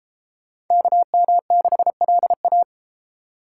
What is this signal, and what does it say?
Morse code, heard as a keyed tone, about 29 wpm, 700 Hz. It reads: KM6LA